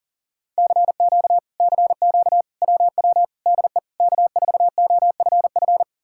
Morse code, heard as a keyed tone, about 54 wpm, spaced as for 47 wpm, 700 Hz. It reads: CQ CQ WW DE K4OFF
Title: Speed-Racing Common QSO Phrases - 40wpm